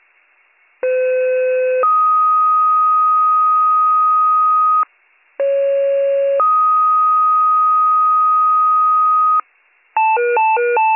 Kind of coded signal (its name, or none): two-tone page